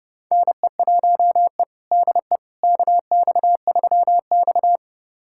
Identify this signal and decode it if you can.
Morse code, heard as a keyed tone, about 30 wpm, 700 Hz. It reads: NE1E DE KX3X
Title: USA Callsign DE Callsign - 4 Characters - 30wpm